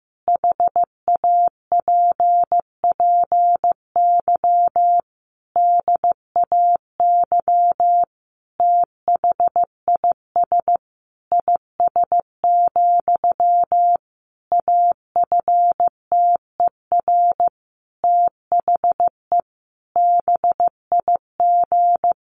Morse code, heard as a keyed tone, about 15 wpm, 700 Hz. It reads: HAPPY DAY THIS IS, AFTER THE BIG
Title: Sentences from Top 1500 Words 15wpm